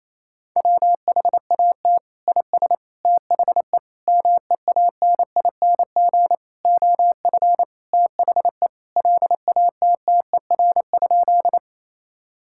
Morse code, heard as a keyed tone, about 28 wpm, 700 Hz. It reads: WHAT IS THE MEANING OF THE LATTER?
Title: Sentences from Top 1600 Words 28wpm